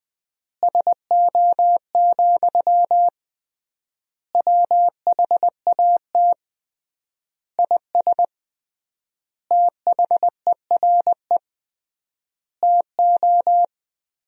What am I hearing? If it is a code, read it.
Morse code, heard as a keyed tone, about 19 wpm, 700 Hz. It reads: SO, WHAT IS THERE TO
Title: 3x Word-Spacing - Sentences from Top 200 Words - 20wpm